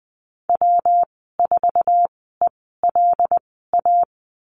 Morse code, heard as a keyed tone, about 20 wpm, spaced as for 15 wpm, 700 Hz. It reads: W4ELA
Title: 167 Lesson 27g - Number 7 - Call Signs 20wpm